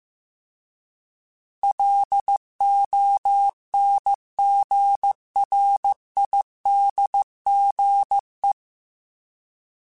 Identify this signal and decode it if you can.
Morse code, heard as a keyed tone, about 15 wpm, 800 Hz. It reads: LONGRIDGE